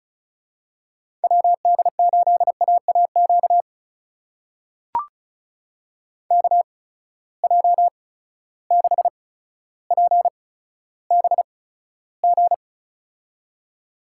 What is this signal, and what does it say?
Morse code, heard as a keyed tone, about 36 wpm, spaced as for 15 wpm, 700 Hz. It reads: WD8AAQ KJ6PBG
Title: USA Call Signs - 6 Characters - 8x Character Spacing - 35wpm